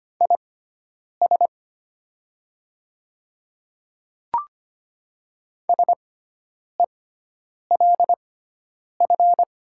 Morse code, heard as a keyed tone, about 25 wpm, spaced as for 9 wpm, 700 Hz. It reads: IS SELF
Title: 6x Character Spacing - Top 300 Words - 25wpm